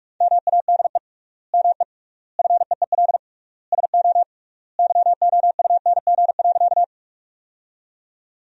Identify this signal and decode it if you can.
Morse code, heard as a keyed tone, about 42 wpm, 700 Hz. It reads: MADE ME FEEL SO YOUNG.
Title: 3x Word-Spacing - Sentences from Top 400 Words - 45wpm